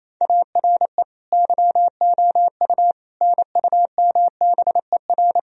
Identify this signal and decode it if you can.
Morse code, heard as a keyed tone, about 28 wpm, 700 Hz. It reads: ARE YOU NUMBER
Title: Sentences from Top 1000 Words - Review Part 1 - 28wpm